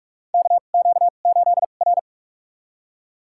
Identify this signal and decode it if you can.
Morse code, heard as a keyed tone, about 46 wpm, spaced as for 32 wpm, 700 Hz. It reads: KQ8R
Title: USA Call Signs - 2x Character Spacing - 45wpm